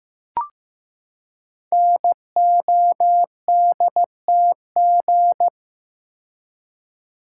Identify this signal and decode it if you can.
Morse code, heard as a keyed tone, about 15 wpm, 700 Hz. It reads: NODTG
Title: Code Groups - 5 Letters - 15wpm